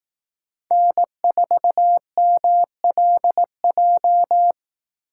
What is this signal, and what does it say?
Morse code, heard as a keyed tone, about 18 wpm, 700 Hz. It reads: N4MLJ
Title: Speed Racing - USA 1x3 Call Signs 15wpm